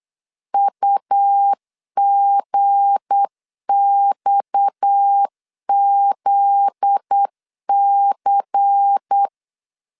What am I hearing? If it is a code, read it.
Morse code, heard as a keyed tone, about 8 wpm, 800 Hz. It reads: UGXZC
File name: CW / DET / 1 9 1500C S C 8wpm 5653